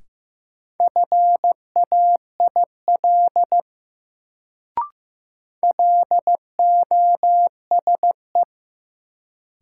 Morse code, heard as a keyed tone, about 15 wpm, 700 Hz. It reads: FAIL LOSE
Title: Sentences from Top 1100 Words - New Word Drill - 15wpm